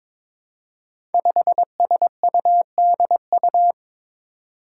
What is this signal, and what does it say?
Morse code, heard as a keyed tone, about 22 wpm, 700 Hz. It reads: 5SUDU
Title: Five Challenging Letter-Number Combinations 22wpm